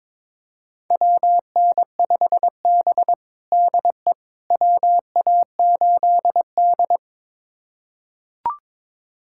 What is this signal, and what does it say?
Morse code, heard as a keyed tone, about 22 wpm, 700 Hz. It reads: WN5B DE WA8D
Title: USA Callsign DE Callsign - 4 Characters - 22wpm